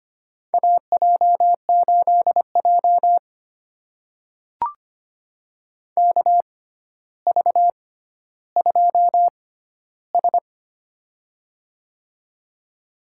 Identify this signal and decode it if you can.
Morse code, heard as a keyed tone, about 25 wpm, spaced as for 13 wpm, 700 Hz. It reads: AJ8J KV2S
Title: USA Call Signs - 4 Characters - 6x Character Spacing - 25wpm